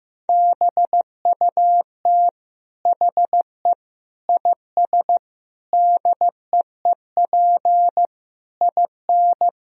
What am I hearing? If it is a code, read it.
Morse code, heard as a keyed tone, about 16 wpm, 700 Hz. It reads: BUT HE IS DEEP IN
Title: Sentences from Top 1000 Words - Review Part 3 - 15wpm